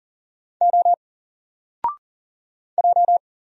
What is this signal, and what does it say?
Morse code, heard as a keyed tone, about 40 wpm, 700 Hz. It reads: O J
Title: Single Character (Letter:Number:Punctuation:Prosign) 40wpm